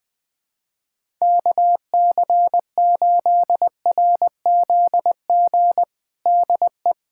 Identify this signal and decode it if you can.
Morse code, heard as a keyed tone, about 20 wpm, 700 Hz. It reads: KC8RZG DE
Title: USA Callsign DE Callsign 20wpm